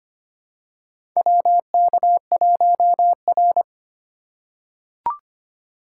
Morse code, heard as a keyed tone, about 25 wpm, 700 Hz. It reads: WK1R
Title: USA Call Signs - 4 Characters - 3x Character Spacing - 25wpm